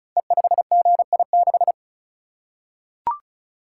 Morse code, heard as a keyed tone, about 36 wpm, 700 Hz. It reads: E5GI6
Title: Code Groups - 5 Letter:Numbers - 35wpm